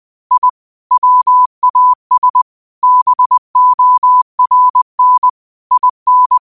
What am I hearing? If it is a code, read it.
Morse code, heard as a keyed tone, about 20 wpm, 1000 Hz. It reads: I WAS BORN IN